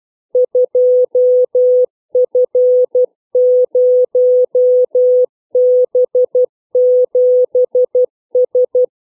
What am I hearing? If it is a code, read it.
Morse code, heard as a keyed tone, about 12 wpm, 500 Hz. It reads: 2F0B7S
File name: CW / letter number groups 3